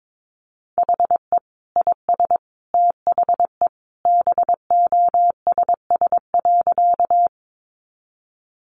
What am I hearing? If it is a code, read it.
Morse code, heard as a keyed tone, about 22 wpm, 700 Hz. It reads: HE IS THE BOSS.
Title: Sentences from Top 700 Words 22wpm